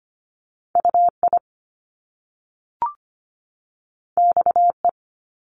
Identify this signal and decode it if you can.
Morse code, heard as a keyed tone, about 25 wpm, 700 Hz. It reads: UI XE
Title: Top Two Letter Combinations in Words 25wpm